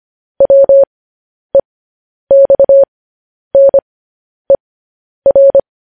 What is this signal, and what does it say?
Morse code, heard as a keyed tone, about 25 wpm, spaced as for 10 wpm, 550 Hz. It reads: WEXNER
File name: podcast / News headlines-1x-25x10